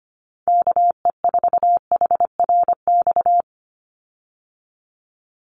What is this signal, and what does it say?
Morse code, heard as a keyed tone, about 25 wpm, 700 Hz. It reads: KE4HRX
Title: USA Call Signs 25wpm